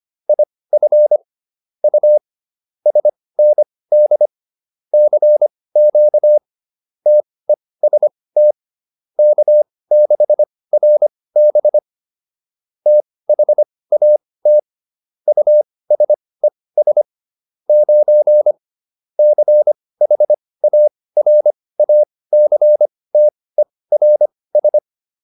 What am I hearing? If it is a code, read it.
Morse code, heard as a keyed tone, about 25 wpm, spaced as for 18 wpm, 600 Hz. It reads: IF U SND CQ TEST K6RB THAT USES 9 CHARACTERS